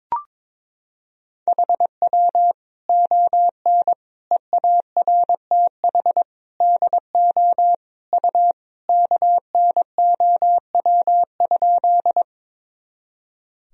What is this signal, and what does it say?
Morse code, heard as a keyed tone, about 21 wpm, 700 Hz. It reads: HW ON EARTH DO U KNOW?
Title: Sentences from Top 200 Words and Abbreviations 22wpm